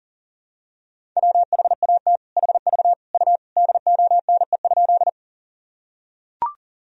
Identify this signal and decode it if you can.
Morse code, heard as a keyed tone, about 40 wpm, 700 Hz. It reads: WHAT HV U DONE?
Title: Sentences from Top 500 Words and Abbreviations 40wpm